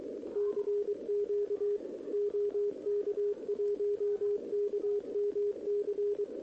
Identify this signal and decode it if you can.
Morse code, heard as a keyed tone, about 23 wpm, 410 Hz, with radio noise. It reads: CQ OK1KMC
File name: OK QRP 2026 / OKQRP2026